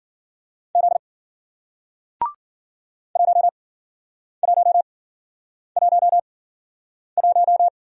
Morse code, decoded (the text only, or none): Z 1111